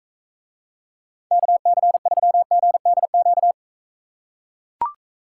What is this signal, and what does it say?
Morse code, heard as a keyed tone, about 43 wpm, 700 Hz. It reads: KC3GDQ